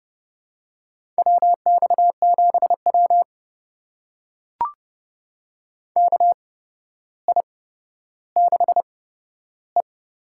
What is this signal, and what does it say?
Morse code, heard as a keyed tone, about 31 wpm, spaced as for 13 wpm, 700 Hz. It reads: WX7W KI6E
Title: USA Call Signs - 4 Characters - 8x Character Spacing - 30wpm